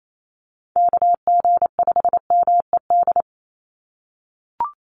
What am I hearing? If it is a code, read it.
Morse code, heard as a keyed tone, about 28 wpm, 700 Hz. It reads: KG5MED
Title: USA Call Signs 28wpm